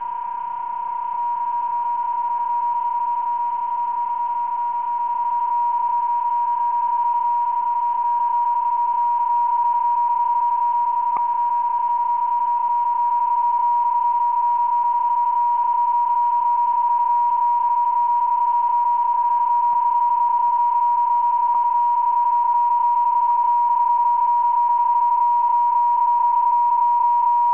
PSK31, about 31.25 baud.